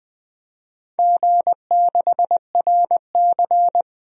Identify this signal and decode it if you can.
Morse code, heard as a keyed tone, about 21 wpm, 700 Hz. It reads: G6RC
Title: DXCC-100 Call Signs - 4 Characters - 20wpm